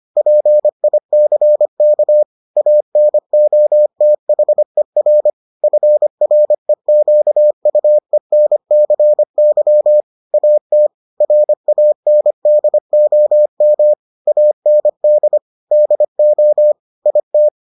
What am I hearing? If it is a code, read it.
Morse code, heard as a keyed tone, about 25 wpm, 600 Hz. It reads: PICK ANOTHER FREQUENCY AT RANDOM AND DO IT